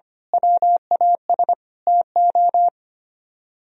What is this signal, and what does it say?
Morse code, heard as a keyed tone, about 29 wpm, 700 Hz. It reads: WAS TO
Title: Speed-Racing - Sets of 5 Words 25wpm